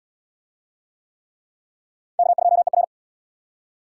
Morse code, heard as a keyed tone, about 77 wpm, 700 Hz. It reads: K2ER